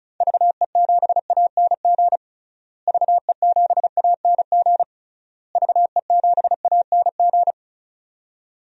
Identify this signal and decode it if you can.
Morse code, heard as a keyed tone, about 36 wpm, 700 Hz. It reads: VE7ANG VE7ANG VE7ANG
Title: DXCC-100 Call Signs - 6 Characters - Repeated 3x - 35wpm